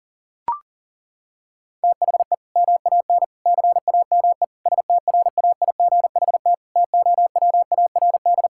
Morse code, heard as a keyed tone, about 40 wpm, 700 Hz. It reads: THE MAN CAME STRAIGHT TOWARD
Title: Sentences from Top 1000 Words - Review Part 6 - 40wpm